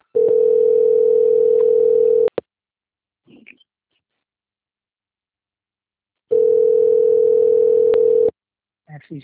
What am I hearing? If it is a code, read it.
Ringback tone (North America), 440 + 480 Hz.
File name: out- - - - - .11552